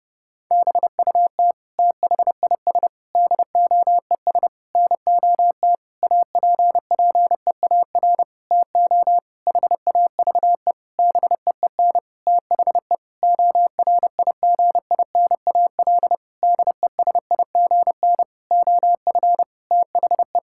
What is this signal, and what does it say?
Morse code, heard as a keyed tone, about 30 wpm, 700 Hz. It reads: BUT THIS DOES NOT APPEAR TO HAVE BEEN THE ORIGINAL DESIGN OF THE
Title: Sentences from Top 1000 Words - Review Part 6 - 30wpm